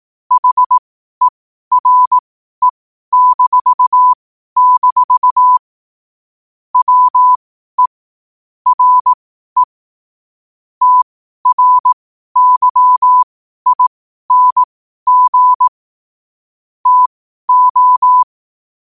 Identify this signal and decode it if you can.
Morse code, heard as a keyed tone, about 18 wpm, spaced as for 12 wpm, 1000 Hz. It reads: HERE-- WE RE TRYING TO